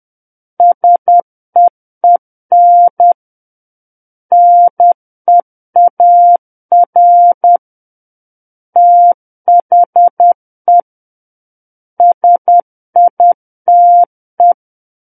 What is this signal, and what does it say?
Morse code, heard as a keyed tone, about 10 wpm, 700 Hz. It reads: SEEN NEAR THE SITE
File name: Book One / 01-Chapter 1 - 10 WPM